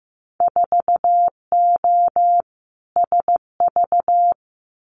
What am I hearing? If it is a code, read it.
Morse code, heard as a keyed tone, about 15 wpm, 700 Hz. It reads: 4O SV
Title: Winter Field Day Exchanges 15wpm